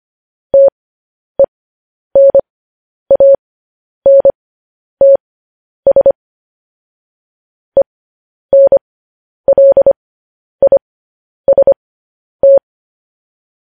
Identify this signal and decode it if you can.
Morse code, heard as a keyed tone, about 25 wpm, spaced as for 10 wpm, 550 Hz. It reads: TENANTS ENLIST
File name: podcast / News headlines-1x-25x10